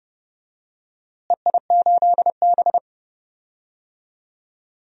Morse code, heard as a keyed tone, about 30 wpm, 700 Hz. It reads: EI8B
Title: DXCC-100 Call Signs - 4 Characters - Repeated 3x - 30wpm